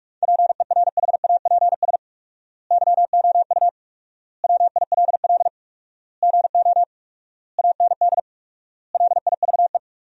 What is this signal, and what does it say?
Morse code, heard as a keyed tone, about 41 wpm, 700 Hz. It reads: PERHAPS YOU WILL GO AND LIVE
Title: 4x Word-Spacing - Sentences from Top 500 Words - 45wpm